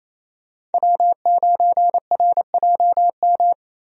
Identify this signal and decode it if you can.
Morse code, heard as a keyed tone, about 28 wpm, 700 Hz. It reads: W9RJM
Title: USA Call Signs - 5 Characters - Repeated 2x - 28wpm